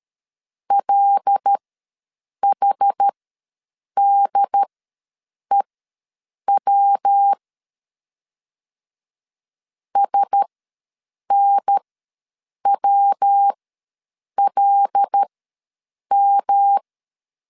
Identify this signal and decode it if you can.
Morse code, heard as a keyed tone, about 13 wpm, spaced as for 7 wpm, 800 Hz. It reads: LHDEW SNWLM
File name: LEZIONI / 5 LBVC / 5 L ws 12wpm 2958